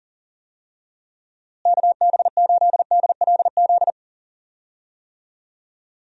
Morse code, heard as a keyed tone, about 40 wpm, 700 Hz. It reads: KB8DLZ
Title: USA Call Signs - 6 Characters - Repeated 3x - 40wpm